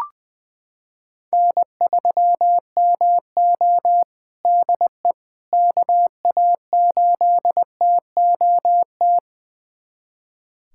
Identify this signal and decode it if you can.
Morse code, heard as a keyed tone, about 20 wpm, 700 Hz. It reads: N3MO DE KA8TOT
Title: USA Callsign DE Callsign 20wpm